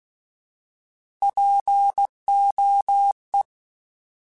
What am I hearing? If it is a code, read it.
Morse code, heard as a keyed tone, about 16 wpm, 800 Hz. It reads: POE
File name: Alphabetical / Alphabetical mp3s / Ps @15